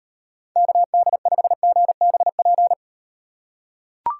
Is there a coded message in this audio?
Yes: Morse code from 0.6 to 2.7 s, heard as a keyed tone, about 33 wpm, 700 Hz. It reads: KD5GBP